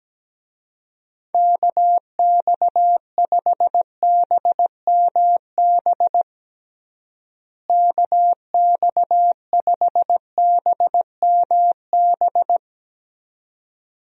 Morse code, heard as a keyed tone, about 17 wpm, 700 Hz. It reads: KX5BMB KX5BMB
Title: USA Call Signs - 6 Characters - Repeated 2x - 17wpm